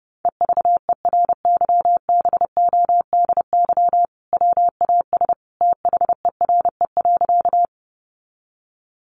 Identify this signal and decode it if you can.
Morse code, heard as a keyed tone, about 28 wpm, 700 Hz. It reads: EVERYBODY WAS THERE.
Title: Sentences from Top 1400 Words 30wpm